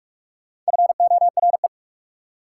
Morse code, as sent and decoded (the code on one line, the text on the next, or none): ..-. --- .-. .
FORE